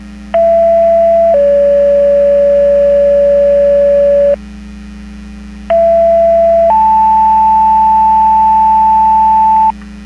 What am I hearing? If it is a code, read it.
Two-tone page, heard as a one-second tone, then a three-second tone, about 670 then 570 Hz, about 670 then 880 Hz. It is heard from 0.3 to 9.7 s.
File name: Portage 154.385 Paging